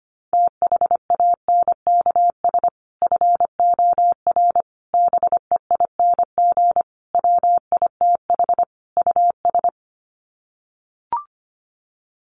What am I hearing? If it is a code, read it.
Morse code, heard as a keyed tone, about 25 wpm, 700 Hz. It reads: THANKS FOR BEING WITH US
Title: Sets of 5 Words 25wpm